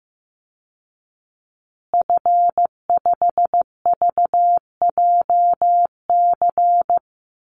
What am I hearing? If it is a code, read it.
Morse code, heard as a keyed tone, about 15 wpm, 700 Hz. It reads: F5VJC
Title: Call Signs Repeated 2x - 15wpm